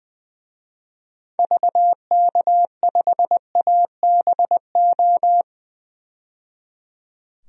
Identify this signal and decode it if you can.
Morse code, heard as a keyed tone, about 21 wpm, 700 Hz. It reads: VK5ABO
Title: DXCC-100 Call Signs - 6 Characters - 20wpm